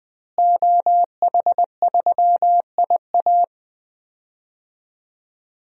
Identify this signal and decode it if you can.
Morse code, heard as a keyed tone, about 21 wpm, 700 Hz. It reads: OH3IA
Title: DXCC-100 Call Signs - 5 Characters - Repeated 3x - 20wpm